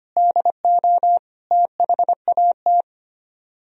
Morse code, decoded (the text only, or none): DO THAT